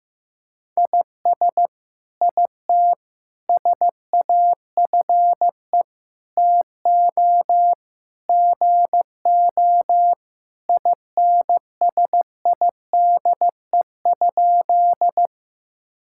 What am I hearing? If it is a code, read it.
Morse code, heard as a keyed tone, about 15 wpm, 700 Hz. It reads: IS IT SAFE TO GO INSIDE?
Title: Sentences from Top 1000 Words - Review Part 6 - 15wpm